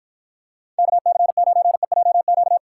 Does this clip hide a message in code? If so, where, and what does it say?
Yes: Morse code from 0.8 to 2.6 s, heard as a keyed tone, about 44 wpm, 700 Hz. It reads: KC9EJQ